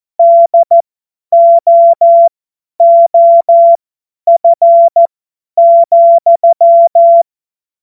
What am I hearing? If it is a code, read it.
Morse code, heard as a keyed tone, about 14 wpm, spaced as for 10 wpm, 680 Hz. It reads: DOOF,
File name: Morsekurs / Witze / Witz 28. Dezember 2021, 50 F70